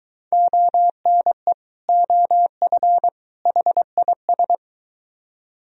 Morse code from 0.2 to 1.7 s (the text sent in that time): ONE